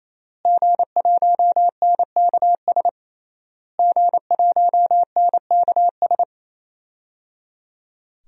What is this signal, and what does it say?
Morse code, heard as a keyed tone, about 28 wpm, 700 Hz. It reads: G1NKS G1NKS
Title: DXCC-100 Call Signs - 5 Characters - Repeated 3x - 28wpm